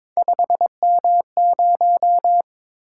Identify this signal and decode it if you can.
Morse code, heard as a keyed tone, about 22 wpm, 700 Hz. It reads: 5M0
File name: practice / 22wpm / Three Letter-Number Combinations - 22wpm